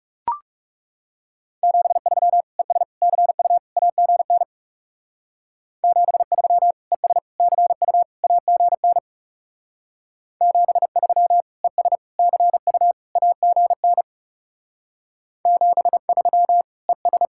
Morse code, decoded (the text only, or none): 73 ES CU AGN 73 ES CU AGN 73 ES CU AGN 73 ES